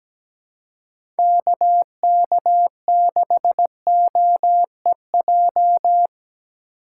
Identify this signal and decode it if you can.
Morse code, heard as a keyed tone, about 17 wpm, 700 Hz. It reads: KK6OEJ
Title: USA Call Signs - 3x Character Spacing - 17wpm